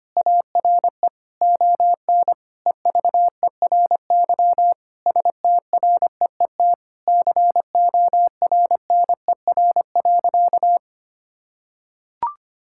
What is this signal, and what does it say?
Morse code, heard as a keyed tone, about 25 wpm, 700 Hz. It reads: ARE ON EVERY STREET CORNER.
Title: Sentences from Top 1600 Words 25wpm